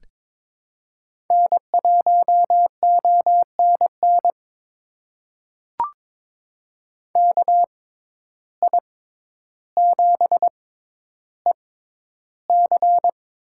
Morse code, heard as a keyed tone, about 22 wpm, spaced as for 11 wpm, 700 Hz. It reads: N1ONN KI7EC
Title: USA Call Signs - 5 Characters - 6x Character Spacing - 22wpm